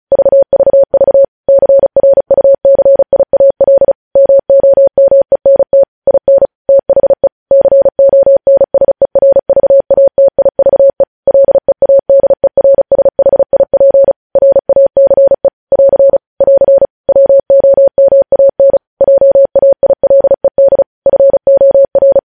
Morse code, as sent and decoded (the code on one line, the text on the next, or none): ...- ...- ...- / -.-. .-. ..- -.-. .. .- .-.. / -- --- -- . -. - / .. -. / - .... / -.-. --- -. ... .-. ...- .- - .. ...- . / .-.. .- -.. .-. ... .... .. .--. / .-. .- -.-. / .-.-. / .-.-. / .-- --- -- .- -. / .--- .- .. .-.. -.. / ..-. --- .-.
VVV CRUCIAL MOMENT IN THE CONSERVATIVE LEADERSHIP RACE <AR> <AR> WOMAN JAILED FOR